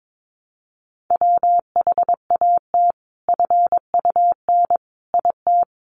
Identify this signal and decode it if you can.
Morse code, heard as a keyed tone, about 22 wpm, 700 Hz. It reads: WHAT FUN IT
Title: Sentences from Top 700 Words 22wpm